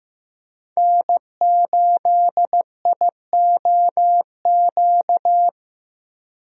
Morse code, heard as a keyed tone, about 15 wpm, 700 Hz. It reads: N8IOQ
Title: USA Call Signs - 5 Characters - Repeated 2x - 15wpm